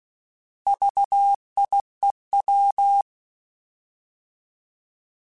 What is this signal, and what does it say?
Morse code, heard as a keyed tone, about 16 wpm, 800 Hz. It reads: VIEW